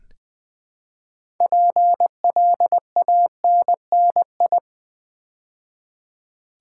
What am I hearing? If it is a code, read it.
Morse code, heard as a keyed tone, about 20 wpm, 700 Hz. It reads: PLANNI
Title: Build A Word - 2 Letters At A Time - 8-Letter Words - 20wpm